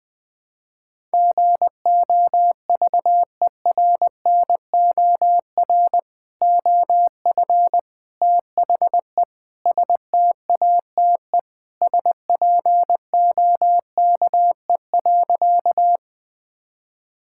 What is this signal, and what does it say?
Morse code, heard as a keyed tone, about 20 wpm, 700 Hz. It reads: GOVERNOR OF THE STATE SPOKE.
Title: Sentences with 3-Word Phrases - 20wpm